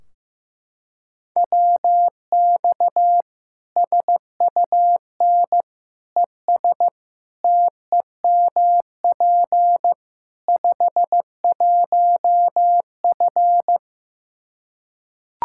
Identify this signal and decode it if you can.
Morse code, heard as a keyed tone, about 15 wpm, 700 Hz. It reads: WX SUN ES TEMP 51F
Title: Practice QSO - Exchange 2 - Info - 15wpm